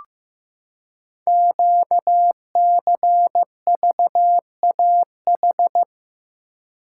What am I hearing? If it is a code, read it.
Morse code, heard as a keyed tone, about 15 wpm, 700 Hz. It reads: QCVAH
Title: Code Groups - 5 Letters - 15wpm